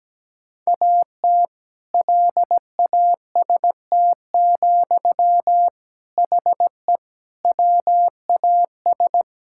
Morse code, heard as a keyed tone, about 17 wpm, 700 Hz. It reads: AT LAST, HE WAS
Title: Sentences from Top 1000 Words - Review Part 6 - 17wpm